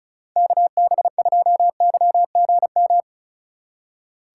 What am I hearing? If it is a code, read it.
Morse code, heard as a keyed tone, about 35 wpm, 700 Hz. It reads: KB2YGM
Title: USA Call Signs - 6 Characters - 3x Character Spacing - 35wpm